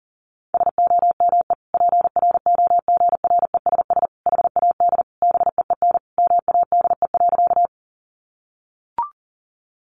Morse code, heard as a keyed tone, about 40 wpm, 700 Hz. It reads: SOME PROGRESS HAD BEEN MADE.
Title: Sentences from Top 1400 Words 40wpm